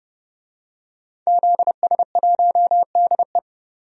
Morse code, heard as a keyed tone, about 32 wpm, 700 Hz. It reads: ZS1DE
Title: DXCC-100 Call Signs - 5 Characters - 30wpm